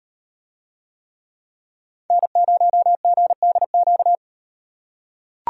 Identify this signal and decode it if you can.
Morse code, heard as a keyed tone, about 38 wpm, 700 Hz. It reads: N0GDQ